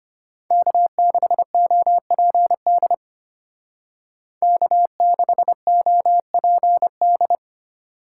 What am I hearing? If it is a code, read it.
Morse code, heard as a keyed tone, about 30 wpm, 700 Hz. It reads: K6OPD K6OPD
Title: Speed Racing - USA 1x3 Call Signs 25wpm